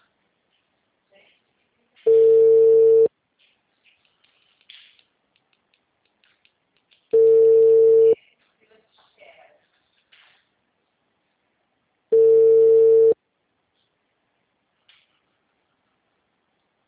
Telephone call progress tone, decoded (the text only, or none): ringback tone